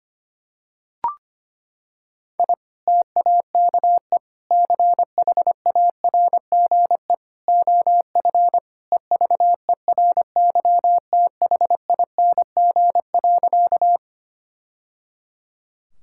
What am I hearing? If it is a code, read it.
Morse code, heard as a keyed tone, about 25 wpm, 700 Hz. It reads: I TAKE CHARGE OF EVERYTHING.
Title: Sentences from Top 1000 Words - Review Part 6 - 25wpm